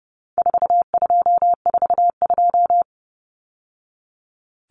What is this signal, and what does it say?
Morse code, heard as a keyed tone, about 30 wpm, 700 Hz. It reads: 4242
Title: Numbers - 4-Digit 30wpm